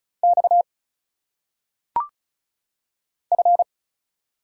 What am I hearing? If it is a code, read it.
Morse code, heard as a keyed tone, about 36 wpm, 700 Hz. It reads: X F
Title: Single Letters - Phonetic Pronunciation 35wpm